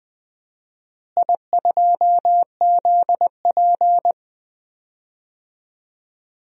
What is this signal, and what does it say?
Morse code, heard as a keyed tone, about 20 wpm, 700 Hz. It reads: I2ZP
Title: DXCC-100 Call Signs - 4 Characters - Repeated 3x - 20wpm